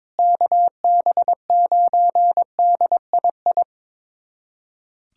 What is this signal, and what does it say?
Morse code, heard as a keyed tone, about 22 wpm, 700 Hz. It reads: KB9DII
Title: USA Callsign DE Callsign - 6 Characters - 22wpm